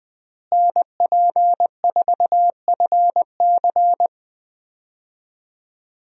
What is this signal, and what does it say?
Morse code, heard as a keyed tone, about 20 wpm, 700 Hz. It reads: NP4FC
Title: USA Call Signs - 5 Characters - Repeated 3x - 20wpm